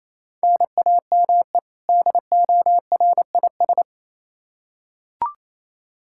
Morse code, heard as a keyed tone, about 28 wpm, 700 Hz. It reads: NAME DORIS
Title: Practice QSO - Exchange 1 - Beginning - 28wpm